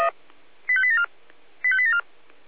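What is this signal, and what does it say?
Five-tone selective call (ZVEI), heard as quick runs of five tones, tones of about 70 ms. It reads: 85874 85874
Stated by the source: ZVEI five-tone call 85840 received by SDR.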